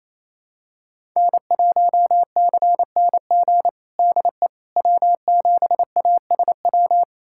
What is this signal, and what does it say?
Morse code, heard as a keyed tone, about 28 wpm, 700 Hz. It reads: N1CNG DE W7ASW
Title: USA Callsign DE Callsign - 5 Characters - 28wpm